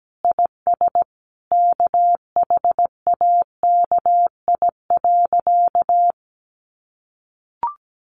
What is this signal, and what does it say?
Morse code, heard as a keyed tone, about 17 wpm, 700 Hz. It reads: IS KHAKI.
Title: Additional Easy Phrases 17wpm